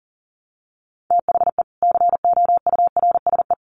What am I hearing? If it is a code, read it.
Morse code, heard as a keyed tone, about 40 wpm, 700 Hz. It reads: THE COURSE
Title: Sets of 4 Words 40wpm